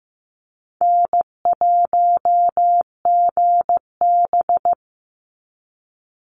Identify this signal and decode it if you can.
Morse code, heard as a keyed tone, about 15 wpm, 700 Hz. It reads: N1GB
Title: NA SOTA Chaser Callsigns 15wpm